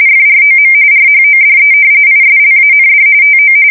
RTTY, about 45.45 baud. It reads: FROM RHODE ISLAND TO
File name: fsk170 / battleshipnj